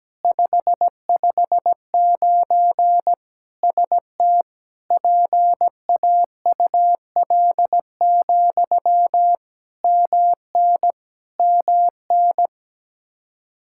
Morse code, heard as a keyed tone, about 17 wpm, 700 Hz. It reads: 559 ST PAUL, MN MN
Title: Practice QSO - Exchange 1 - Beginning - 17wpm